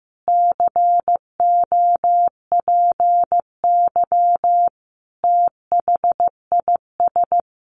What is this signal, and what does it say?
Morse code, heard as a keyed tone, about 15 wpm, 700 Hz. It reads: COPY THIS
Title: Additional Easy Phrases 15wpm